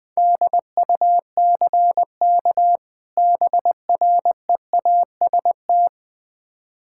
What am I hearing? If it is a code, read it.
Morse code, heard as a keyed tone, about 20 wpm, 700 Hz. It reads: DUCK BREAST
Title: Difficult Words with Hints - 20wpm